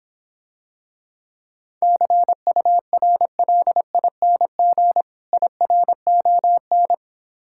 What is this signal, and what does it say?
Morse code, heard as a keyed tone, about 26 wpm, 700 Hz. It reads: CURLING IRON